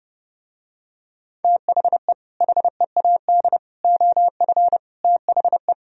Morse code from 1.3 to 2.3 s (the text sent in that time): THE